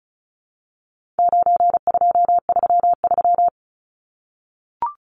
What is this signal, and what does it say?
Morse code, heard as a keyed tone, about 34 wpm, 700 Hz. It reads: 9233
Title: Numbers - 4-Digit 35wpm